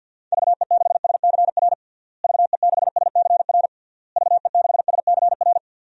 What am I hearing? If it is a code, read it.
Morse code, heard as a keyed tone, about 46 wpm, 700 Hz. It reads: VE6SCR VE6SCR VE6SCR
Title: DXCC-100 Call Signs - 6 Characters - Repeated 3x - 50wpm